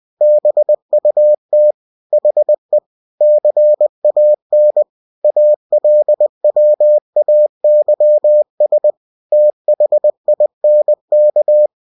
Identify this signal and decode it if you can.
Morse code, heard as a keyed tone, about 20 wpm, 600 Hz. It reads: BUT HE CAN ALWAYS THINK
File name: cw-books / winnie-the-pooh / winnie-20wpm-page-0091